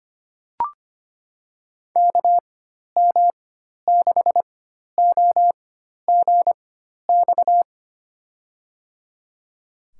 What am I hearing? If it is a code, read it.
Morse code, heard as a keyed tone, about 25 wpm, spaced as for 12 wpm, 700 Hz. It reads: KM6OGX
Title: USA Call Signs - 6 Characters - 4x Character Spacing - 25wpm